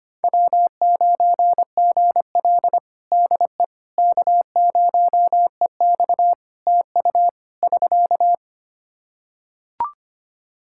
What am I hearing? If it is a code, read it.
Morse code, heard as a keyed tone, about 25 wpm, 700 Hz. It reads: W9GL DE K0EX TU <SK>
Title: Practice QSO - Exchange 4 - Ending - 25wpm